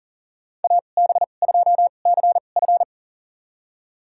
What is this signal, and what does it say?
Morse code, heard as a keyed tone, about 40 wpm, spaced as for 29 wpm, 700 Hz. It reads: AB2CF
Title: USA Call Signs - 5 Characters - 2x Character Spacing - 40wpm